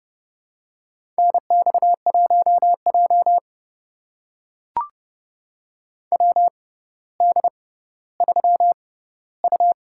Morse code, heard as a keyed tone, about 30 wpm, spaced as for 15 wpm, 700 Hz. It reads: NX1J WD3U
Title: USA Call Signs - 4 Characters - 6x Character Spacing - 30wpm